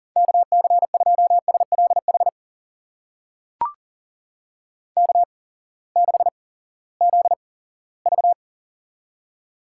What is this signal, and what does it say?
Morse code, heard as a keyed tone, about 37 wpm, spaced as for 20 wpm, 700 Hz. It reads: KC2SLH K6ZV
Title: USA Call Signs - 8x Character Spacing - 40wpm